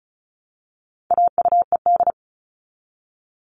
Morse code, heard as a keyed tone, about 34 wpm, 700 Hz. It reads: AUED
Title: All Cut Numbers - 4-Digit 35wpm